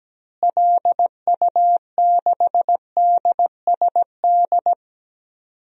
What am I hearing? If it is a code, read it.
Morse code, heard as a keyed tone, about 17 wpm, 700 Hz. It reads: LU6DSD
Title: DXCC-100 Call Signs - 6 Characters - Repeated 3x - 17wpm